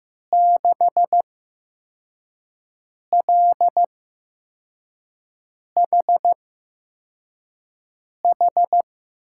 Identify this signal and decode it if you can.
Morse code, heard as a keyed tone, about 15 wpm, spaced as for 4 wpm, 700 Hz. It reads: 6LHH
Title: USA Call Signs - 6 Characters - 8x Character Spacing - 15wpm